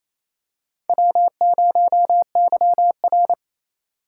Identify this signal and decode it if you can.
Morse code, heard as a keyed tone, about 28 wpm, 700 Hz. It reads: W0YR
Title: K1USN SST QSO Practice - 28wpm